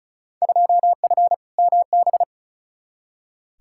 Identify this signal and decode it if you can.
Morse code, heard as a keyed tone, about 36 wpm, 700 Hz. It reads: 2F MB
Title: Field Day Exchanges Encore - 35wpm